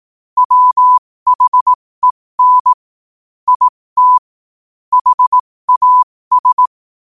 Morse code, heard as a keyed tone, about 18 wpm, spaced as for 15 wpm, 1000 Hz. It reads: WHEN IT HAS
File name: qotdfiles / 15wpm